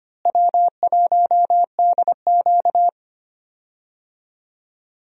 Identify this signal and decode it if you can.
Morse code, heard as a keyed tone, about 25 wpm, 700 Hz. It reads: W1DQ
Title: USA Call Signs - 4 Characters - Repeated 2x - 25wpm